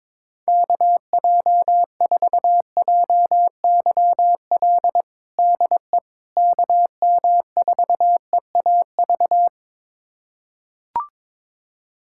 Morse code, heard as a keyed tone, about 22 wpm, 700 Hz. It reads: KJ4JYL DE KM4EAV
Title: USA Callsign DE Callsign 22wpm